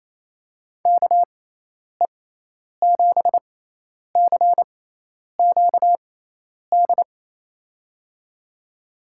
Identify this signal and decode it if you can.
Morse code, heard as a keyed tone, about 28 wpm, spaced as for 10 wpm, 700 Hz. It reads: KE7CQD
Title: USA Call Signs - 6x Character Spacing - 28wpm